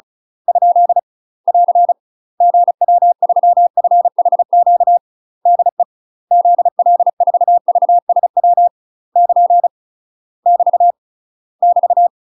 Morse code, heard as a keyed tone, about 36 wpm, 700 Hz. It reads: ? <AR> GW3FHQ DE ZL4VSW <KN> <BT> <BT>
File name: UK MASTER QSOs TRAINING / 35WPM / UK MASTER QSOs 0016